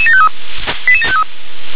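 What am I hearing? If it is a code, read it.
Five-tone selective call (ZVEI), heard as quick runs of five tones, tones of about 70 ms. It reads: 88753
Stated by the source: ZVEI five-tone selective call received on an SDR.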